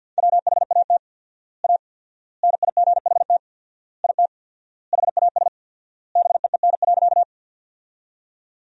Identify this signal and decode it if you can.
Morse code, heard as a keyed tone, about 51 wpm, 700 Hz. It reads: WHAT A NIGHT IT HAS BEEN.
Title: 4x Word-Spacing - Sentences from Top 300 Words - 50wpm